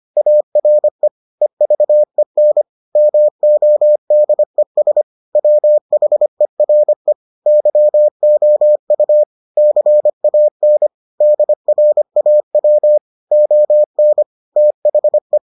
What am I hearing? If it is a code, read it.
Morse code, heard as a keyed tone, about 25 wpm, 600 Hz. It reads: ARE EVEN MODES WHERE YOU CAN DRAW ON THE